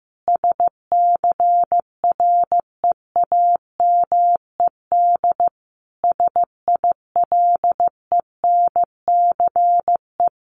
Morse code, heard as a keyed tone, about 15 wpm, 700 Hz. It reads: SCREAMED SILENCE
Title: A Sound Of Thunder 15wpm